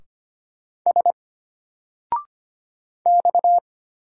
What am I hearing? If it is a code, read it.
Morse code, heard as a keyed tone, about 25 wpm, 700 Hz. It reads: S X
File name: Single-Letter-Number-25wpm